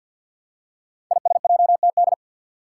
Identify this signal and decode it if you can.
Morse code, heard as a keyed tone, about 53 wpm, 700 Hz. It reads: ISJTD
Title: Five Challenging Letter-Number Combinations 50wpm